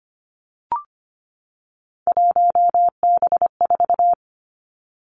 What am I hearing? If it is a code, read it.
Morse code, heard as a keyed tone, about 25 wpm, 700 Hz. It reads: 1B4
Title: Three Challenging Letter-Number Combinations 25wpm